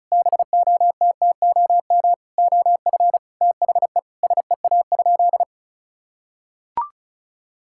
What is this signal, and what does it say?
Morse code, heard as a keyed tone, about 36 wpm, 700 Hz. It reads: BOTTOM OF THE SEA?
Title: Sentences from Top 1000 Words - Review Part 6 - 35wpm